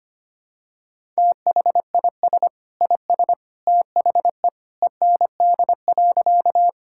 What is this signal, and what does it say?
Morse code, heard as a keyed tone, about 25 wpm, 700 Hz. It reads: THIS IS THE END.
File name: practice / 25wpm / Sentences with 2-Word Phrases - 25wpm